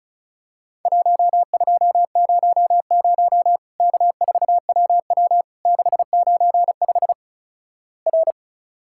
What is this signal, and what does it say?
Morse code, heard as a keyed tone, about 34 wpm, 700 Hz. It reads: 1200 K4WW 695 R
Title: Worked All Europe DX Contest - QTC Exchange - 35wpm